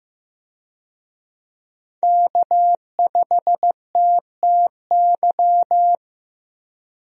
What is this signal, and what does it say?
Morse code, heard as a keyed tone, about 15 wpm, 700 Hz. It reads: K5TTY
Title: USA Call Signs - 5 Characters - Repeated 3x - 15wpm